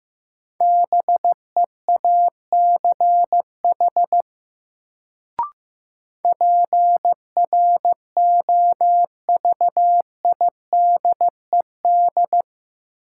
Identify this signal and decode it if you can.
Morse code, heard as a keyed tone, about 15 wpm, 700 Hz. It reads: BEACH PROVIDED
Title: Sentences from Top 1500 Words - New Word Drill Alt - 15wpm